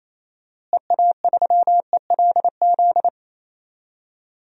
Morse code, heard as a keyed tone, about 28 wpm, 700 Hz. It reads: EA3ELZ
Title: DXCC-100 Call Signs - 6 Characters - Repeated 2x - 28wpm